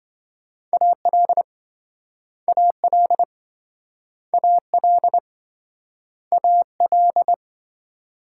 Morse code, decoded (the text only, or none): AL AL AL AL